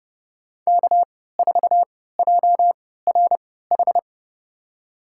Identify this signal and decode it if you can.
Morse code, heard as a keyed tone, about 28 wpm, spaced as for 17 wpm, 700 Hz. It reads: K4JRH
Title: USA Call Signs - 5 Characters - 3x Character Spacing - 30wpm